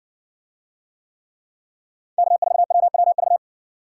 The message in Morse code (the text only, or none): K4WWV